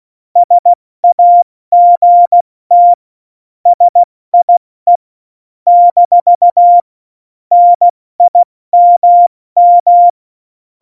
Morse code, heard as a keyed tone, about 16 wpm, 700 Hz. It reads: SAGT SIE - NIMM
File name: Morsekurs / Witze / Witz 14. November 2017, 70 F80